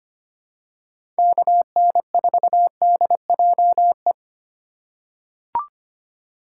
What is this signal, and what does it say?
Morse code, heard as a keyed tone, about 25 wpm, 700 Hz. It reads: KN4DJE
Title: USA Call Signs - 6 Characters - 3x Character Spacing - 25wpm